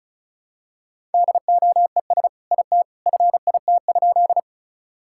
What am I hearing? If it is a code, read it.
Morse code, heard as a keyed tone, about 36 wpm, 700 Hz. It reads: DOES IT FIT?
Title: Sentences from Top 1000 Words - Review Part 5 - 35wpm